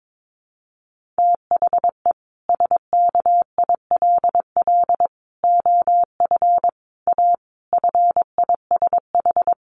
Morse code, heard as a keyed tone, about 22 wpm, 700 Hz. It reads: THE SKILL OF A FISH
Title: Sentences from Top 1400 Words 22wpm